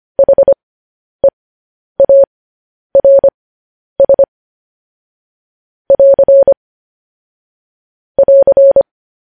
Morse code, heard as a keyed tone, about 25 wpm, spaced as for 10 wpm, 550 Hz. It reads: HEARS <AR> <AR>
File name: podcast / News headlines-1x-25x10